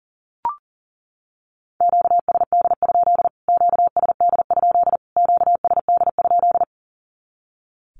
Morse code, heard as a keyed tone, about 40 wpm, 700 Hz. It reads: QSD? QSD? QSD?
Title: Q-Codes Repeated 3x 40wpm